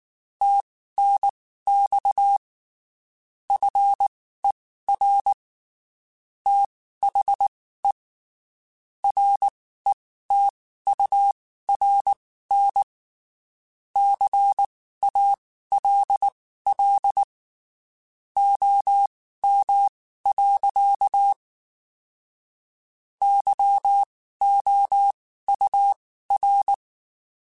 Morse code, heard as a keyed tone, about 19 wpm, spaced as for 14 wpm, 800 Hz. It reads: TNX FER THE RETURN CALL OM. YOUR
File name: (spaced) / QSOs Spaced-mp3s / kc7sac @18 sp